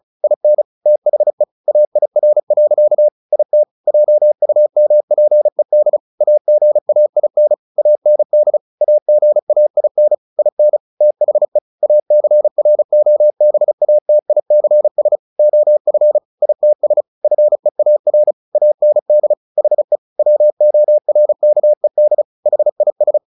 Morse code, heard as a keyed tone, about 34 wpm, 600 Hz. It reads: IN THE AIR. IT JUMPED AGAIN AND AGAIN IN THE ACROBATICS OF ITS FEAR AND HE WORKED HIS
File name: the old man and the sea 2 35wpm